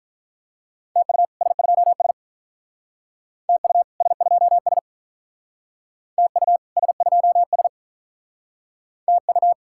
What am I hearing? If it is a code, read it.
Morse code, heard as a keyed tone, about 44 wpm, 700 Hz. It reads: TU S2S TU S2S TU S2S TU